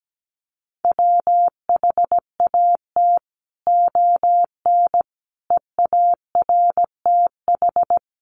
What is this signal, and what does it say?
Morse code, heard as a keyed tone, about 17 wpm, 700 Hz. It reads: WHAT ON EARTH
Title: Sentences from Top 200 Words 17wpm